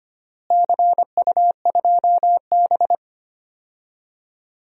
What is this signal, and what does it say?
Morse code, heard as a keyed tone, about 25 wpm, 700 Hz. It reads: CU2B
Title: DXCC-100 Call Signs - 4 Characters - Repeated 3x - 25wpm